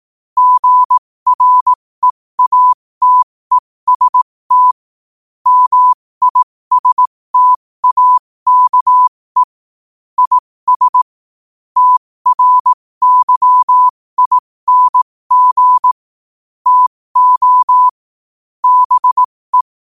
Morse code, heard as a keyed tone, about 18 wpm, spaced as for 15 wpm, 1000 Hz. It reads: GREATEST MISTAKE IS TRYING TO BE
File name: qotdfiles / 15wpm